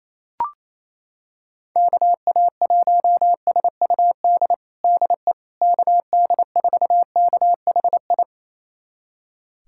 Morse code, heard as a keyed tone, about 28 wpm, 700 Hz. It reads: KA1SUD DE KD4KHI
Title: USA Callsign DE Callsign - 6 Characters - 28wpm